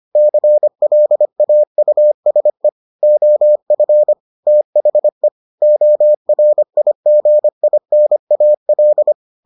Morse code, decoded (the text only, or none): CLAUSE OF THE ORIGINAL